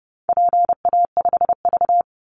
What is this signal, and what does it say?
Morse code, heard as a keyed tone, about 30 wpm, 700 Hz. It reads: PA5V
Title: Call Signs Repeated 3x - 30wpm